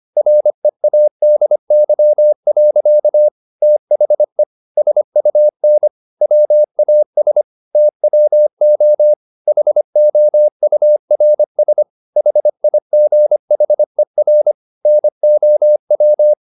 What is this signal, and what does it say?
Morse code, heard as a keyed tone, about 25 wpm, 600 Hz. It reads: READY. THE SUN WAS TWO HOURS HIGHER NOW
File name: the old man and the sea part 1 25wpm